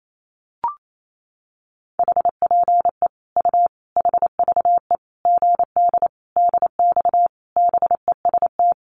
Morse code, heard as a keyed tone, about 28 wpm, 700 Hz. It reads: HPE U HVE GD DX BEST
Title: Common QSO Phrases 28wpm